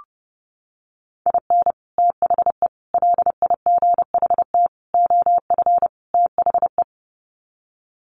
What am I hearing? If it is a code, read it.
Morse code, heard as a keyed tone, about 30 wpm, 700 Hz. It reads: IN THE LIGHT OF THE
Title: Sets of 5 Words Encore2 30wpm